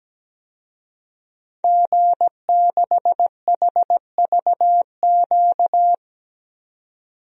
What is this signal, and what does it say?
Morse code, heard as a keyed tone, about 17 wpm, 700 Hz. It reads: G6HVQ
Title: DXCC-100 Call Signs - 5 Characters - 17wpm